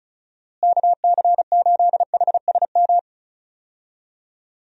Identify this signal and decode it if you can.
Morse code, heard as a keyed tone, about 36 wpm, 700 Hz. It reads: KC8HSM
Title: USA Call Signs - 6 Characters - Repeated 2x - 35wpm